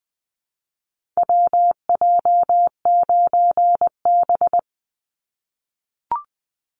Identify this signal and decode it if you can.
Morse code, heard as a keyed tone, about 20 wpm, 700 Hz. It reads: WJ9B
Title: Call Signs - 20wpm